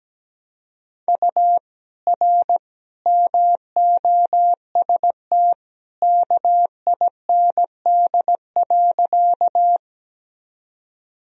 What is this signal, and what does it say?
Morse code, heard as a keyed tone, about 17 wpm, 700 Hz. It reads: U R MOST KIND.
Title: Sentences from Top 200 Words and Abbreviations 17wpm